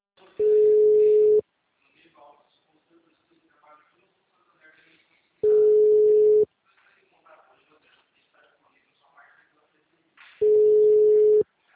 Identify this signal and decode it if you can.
Ringback tone (Europe, 425 Hz), with speech.
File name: out- -2007- - - .37978